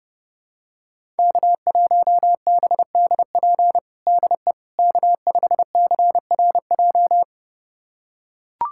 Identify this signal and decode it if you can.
Morse code, heard as a keyed tone, about 30 wpm, 700 Hz. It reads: K1BDP DE K5CRJ
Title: USA Callsign DE Callsign - 5 Characters - 30wpm